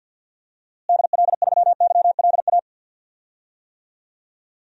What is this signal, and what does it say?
Morse code, heard as a keyed tone, about 51 wpm, 700 Hz. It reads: DL3YLA
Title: DXCC-100 Call Signs - 6 Characters - Repeated 2x - 50wpm